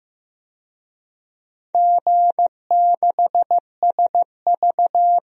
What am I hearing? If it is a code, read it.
Morse code, heard as a keyed tone, about 15 wpm, 700 Hz. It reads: G6SV
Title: DXCC-100 Call Signs - 4 Characters - Repeated 2x - 15wpm